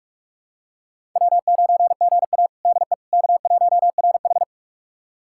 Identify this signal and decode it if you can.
Morse code, heard as a keyed tone, about 47 wpm, 700 Hz. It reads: W9GA DE K1RH
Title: USA Callsign DE Callsign - 4 Characters - 45wpm